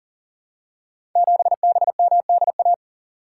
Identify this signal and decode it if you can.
Morse code, heard as a keyed tone, about 37 wpm, 700 Hz. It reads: 7BMDA
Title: Code Groups - 5 Letter:Numbers - 40wpm